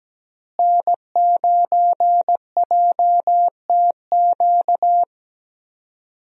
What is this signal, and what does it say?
Morse code, heard as a keyed tone, about 17 wpm, 700 Hz. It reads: N9JTQ
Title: USA Call Signs - 5 Characters - Repeated 3x - 17wpm